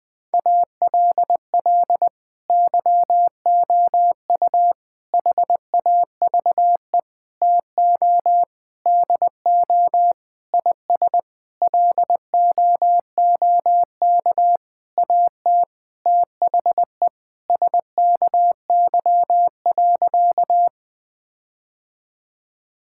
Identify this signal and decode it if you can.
Morse code, heard as a keyed tone, about 20 wpm, 700 Hz. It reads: ALL YOU HAVE TO DO IS LOOK AT THE SKY.
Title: Sentences with 9-Word Phrases - 20wpm